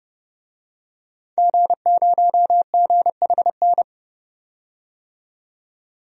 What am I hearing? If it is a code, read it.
Morse code, heard as a keyed tone, about 32 wpm, 700 Hz. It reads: G0GHN
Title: DXCC-100 Call Signs - 5 Characters - 30wpm